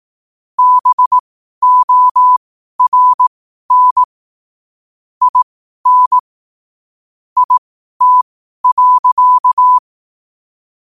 Morse code, heard as a keyed tone, about 18 wpm, spaced as for 12 wpm, 1000 Hz. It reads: BORN IN IT.
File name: qotdfiles / 12wpm